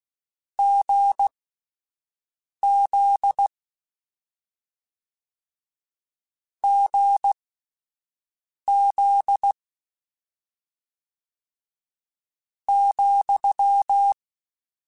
Morse code, heard as a keyed tone, about 16 wpm, spaced as for 5 wpm, 800 Hz. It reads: GZ GZ ,
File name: Difficult Combos / Difficult Combos-mp3s / GZ GZ7 @5